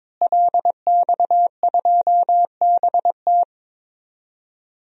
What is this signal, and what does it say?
Morse code, heard as a keyed tone, about 22 wpm, 700 Hz. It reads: LX2BT
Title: DXCC-100 Call Signs - 5 Characters - Repeated 2x - 22wpm